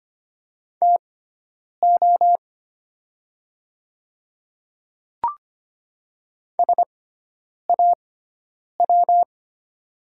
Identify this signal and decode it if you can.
Morse code, heard as a keyed tone, about 25 wpm, spaced as for 9 wpm, 700 Hz. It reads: TO SAW
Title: 6x Character Spacing - Top 500 Words - 25wpm